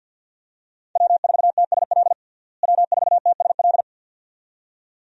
Morse code, heard as a keyed tone, about 52 wpm, 700 Hz. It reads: W4TSL W4TSL
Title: USA Call Signs - 5 Characters - Repeated 2x - 50wpm